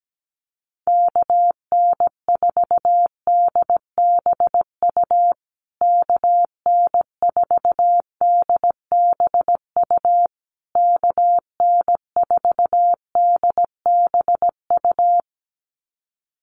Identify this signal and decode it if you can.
Morse code, heard as a keyed tone, about 17 wpm, 700 Hz. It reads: KN4DBU KN4DBU KN4DBU
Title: USA Call Signs Repeated 3x 17wpm